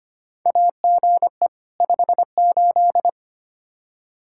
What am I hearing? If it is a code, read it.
Morse code, heard as a keyed tone, about 25 wpm, 700 Hz. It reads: AGE 58
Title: Practice QSO - Exchange 3 - More Info - 25wpm